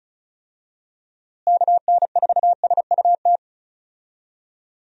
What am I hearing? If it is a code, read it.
Morse code, heard as a keyed tone, about 34 wpm, 700 Hz. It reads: KN4SUT
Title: USA Call Signs - 6 Characters - Repeated 2x - 35wpm